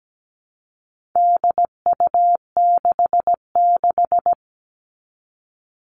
Morse code, heard as a keyed tone, about 17 wpm, 700 Hz. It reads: DU66
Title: All Cut Numbers - 4-Digit 17wpm